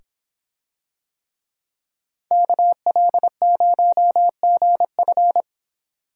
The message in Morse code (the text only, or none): KL0GF